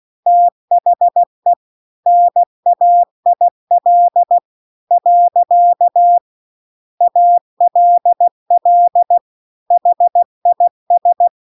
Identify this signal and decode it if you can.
Morse code, heard as a keyed tone, about 16 wpm, 700 Hz. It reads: THE NAIL . ALL HIS